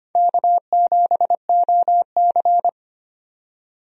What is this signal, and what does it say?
Morse code, heard as a keyed tone, about 25 wpm, 700 Hz. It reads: K7OC
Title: USA Call Signs - 4 Characters - 25wpm